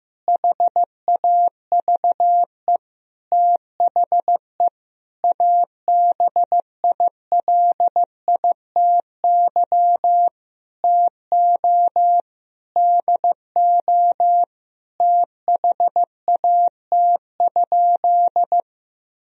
Morse code, heard as a keyed tone, about 16 wpm, 700 Hz. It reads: HAVE THE ABILITY TO DO THAT?
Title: Sentences from Top 1000 Words - Review Part 3 - 15wpm